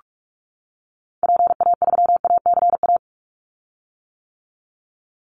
Morse code, heard as a keyed tone, about 46 wpm, 700 Hz. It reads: PA3ACA
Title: Call Signs - 45wpm